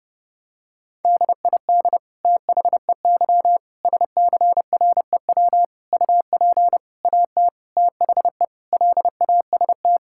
Morse code, heard as a keyed tone, about 32 wpm, 700 Hz. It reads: DID THEY SCREW UP AT THE LAST
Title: Sentences from Top 1600 Words 30wpm